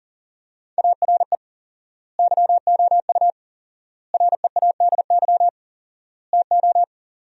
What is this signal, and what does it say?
Morse code, heard as a keyed tone, about 40 wpm, 700 Hz. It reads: ARE YOU READY TO
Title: 4x Word-Spacing - Sentences from Top 400 Words - 40wpm